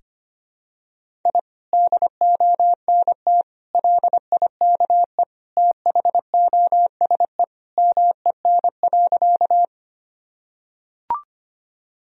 Morse code, heard as a keyed tone, about 25 wpm, 700 Hz. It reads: I DONT LIKE THOSE MEN.
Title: Sentences from Top 1000 Words - Review Part 1 - 25wpm